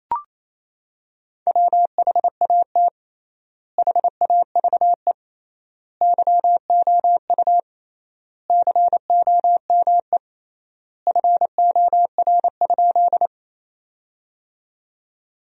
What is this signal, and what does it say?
Morse code, heard as a keyed tone, about 28 wpm, 700 Hz. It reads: WHAT HAVE YOU COME FOR?
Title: 3x Word-Spacing - Sentences from Top 100 Words - 28wpm